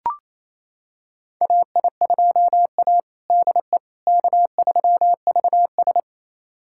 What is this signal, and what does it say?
Morse code, heard as a keyed tone, about 28 wpm, 700 Hz. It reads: AI2A DE K3VS
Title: USA Callsign DE Callsign - 4 Characters - 28wpm